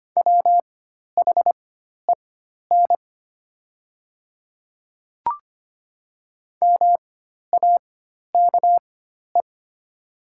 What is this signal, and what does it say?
Morse code, heard as a keyed tone, about 25 wpm, spaced as for 12 wpm, 700 Hz. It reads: WHEN MAKE
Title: 4x Character Spacing - Top 500 Words - 25wpm